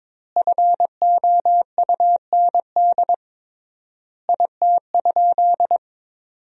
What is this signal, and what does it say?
Morse code, heard as a keyed tone, about 22 wpm, 700 Hz. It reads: FOUND IT?
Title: 3x Word-Spacing - Sentences from Top 300 Words - 22wpm